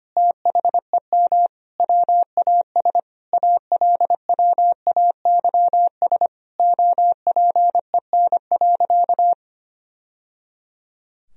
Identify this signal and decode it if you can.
Morse code, heard as a keyed tone, about 25 wpm, 700 Hz. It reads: THEM WAS ALWAYS OPEN.
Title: Sentences from Top 1000 Words - Review Part 1 - 25wpm